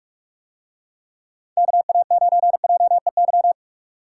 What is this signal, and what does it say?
Morse code, heard as a keyed tone, about 45 wpm, 700 Hz. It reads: KA9JEY